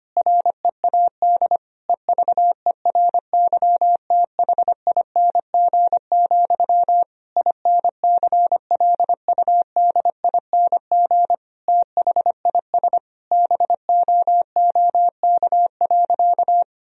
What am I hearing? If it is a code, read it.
Morse code, heard as a keyed tone, about 25 wpm, 700 Hz. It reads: READ EVERYTHING, INCLUDING THIS BOOK.
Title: Sentences from Top 1000 Words - Review Part 4 - 25wpm